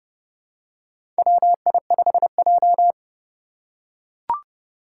Morse code, heard as a keyed tone, about 30 wpm, 700 Hz. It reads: WI5J
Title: Four Challenging Letter-Number Combinations 30wpm